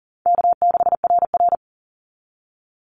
Morse code, heard as a keyed tone, about 40 wpm, 700 Hz. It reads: K6RR
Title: Call Signs Repeated 2x - 40wpm